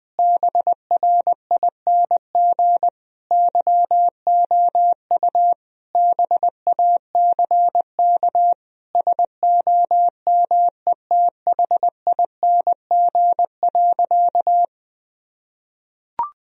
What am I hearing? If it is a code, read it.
Morse code, heard as a keyed tone, about 21 wpm, 700 Hz. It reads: BRING YOU BACK SOMETHING.
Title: Sentences from Top 1000 Words - Review Part 3 - 20wpm